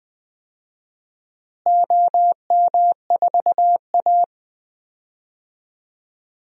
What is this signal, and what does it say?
Morse code, heard as a keyed tone, about 20 wpm, 700 Hz. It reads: OM4A
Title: DXCC-100 Call Signs - 4 Characters - Repeated 2x - 20wpm